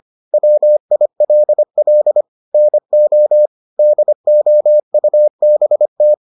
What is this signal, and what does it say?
Morse code, heard as a keyed tone, about 25 wpm, 600 Hz. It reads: WILL NO DOUBT